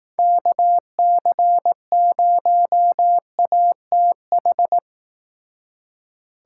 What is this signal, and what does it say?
Morse code, heard as a keyed tone, about 18 wpm, 700 Hz. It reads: KC0ATH
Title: Speed Racing - USA 2x3 Call Signs 15wpm